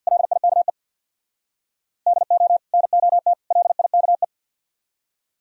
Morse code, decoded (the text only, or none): LIKE DO NOT LIKE